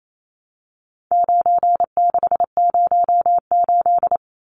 Morse code, heard as a keyed tone, about 28 wpm, 700 Hz. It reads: 9608
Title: Numbers - 4-Digit 28wpm